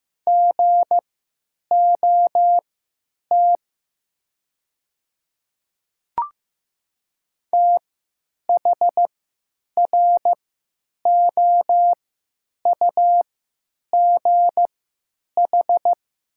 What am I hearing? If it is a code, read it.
Morse code, heard as a keyed tone, about 15 wpm, spaced as for 9 wpm, 690 Hz. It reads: GOT THROUGH
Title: 3x Character Spacing - Top 300 Words - 15wpm